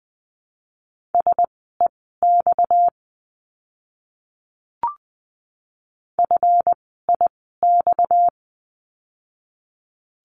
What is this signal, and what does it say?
Morse code, heard as a keyed tone, about 20 wpm, spaced as for 15 wpm, 700 Hz. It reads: SEX FIX
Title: 230 Lesson 37f - Letter x - Words 20wpm